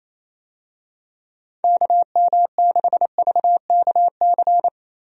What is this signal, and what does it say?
Morse code, heard as a keyed tone, about 27 wpm, 700 Hz. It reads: KM6VKC
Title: USA Call Signs - 6 Characters - Repeated 2x - 28wpm